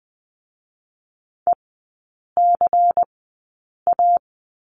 Morse code, heard as a keyed tone, about 20 wpm, spaced as for 8 wpm, 700 Hz. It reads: ECA
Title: 178 Lesson 29e - Slash - Three Characters 20wpm